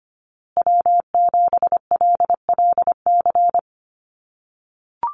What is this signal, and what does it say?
Morse code, heard as a keyed tone, about 25 wpm, 700 Hz. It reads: W7LLC
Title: USA Call Signs Repeated 2x 25wpm